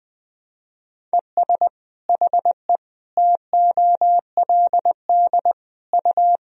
Morse code, heard as a keyed tone, about 20 wpm, 700 Hz. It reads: ES HE TOLD U
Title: Sentences from Top 500 Words and Abbreviations 20wpm